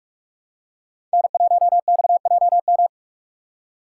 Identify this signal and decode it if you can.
Morse code, heard as a keyed tone, about 46 wpm, 700 Hz. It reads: N1XJM
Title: K1USN SST QSO Practice - 45wpm